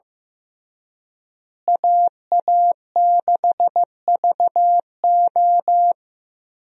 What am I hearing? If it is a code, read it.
Morse code, heard as a keyed tone, about 15 wpm, 700 Hz. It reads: AA6VO
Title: USA Call Signs - 5 Characters - Repeated 2x - 15wpm